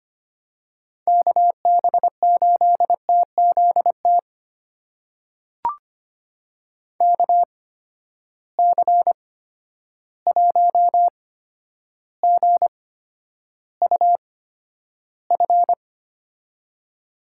Morse code, heard as a keyed tone, about 25 wpm, spaced as for 11 wpm, 700 Hz. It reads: KB8TZT KC1GUF
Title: USA Call Signs - 6 Characters - 8x Character Spacing - 25wpm